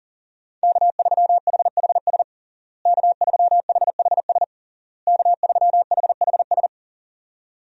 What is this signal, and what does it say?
Morse code, heard as a keyed tone, about 40 wpm, 700 Hz. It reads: K3HHS K3HHS K3HHS
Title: USA Call Signs - 5 Characters - Repeated 3x - 40wpm